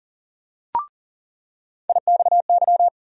Morse code, heard as a keyed tone, about 40 wpm, 700 Hz. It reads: IXY
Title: USA Call Sign Postfixes - 3 Letters 40wpm